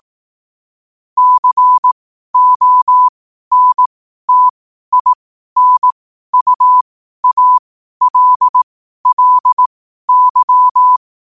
Morse code, heard as a keyed tone, about 18 wpm, spaced as for 12 wpm, 1000 Hz. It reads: CONTINUALLY
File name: qotdfiles / 12wpm